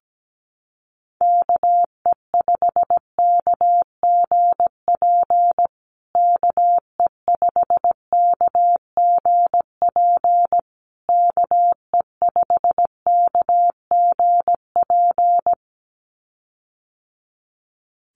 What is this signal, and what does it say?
Morse code, heard as a keyed tone, about 17 wpm, 700 Hz. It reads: KE5KGP KE5KGP KE5KGP
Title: USA Call Signs Repeated 3x 17wpm